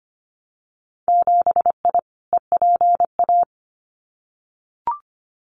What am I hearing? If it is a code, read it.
Morse code, heard as a keyed tone, about 25 wpm, 700 Hz. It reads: 7I EPA
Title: Winter Field Day Exchanges Encore 25wpm